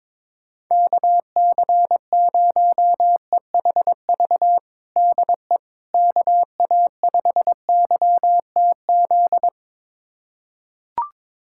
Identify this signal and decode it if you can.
Morse code, heard as a keyed tone, about 22 wpm, 700 Hz. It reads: KC0EHV DE KA5YTZ
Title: USA Callsign DE Callsign - 6 Characters - 22wpm